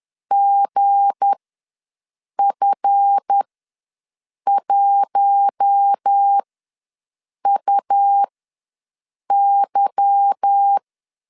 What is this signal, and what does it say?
Morse code, heard as a keyed tone, about 10 wpm, spaced as for 6 wpm, 800 Hz. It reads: GF1UY
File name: LEZIONI / 8 YJ45 / 8 YJ45 ws 10wpm 3843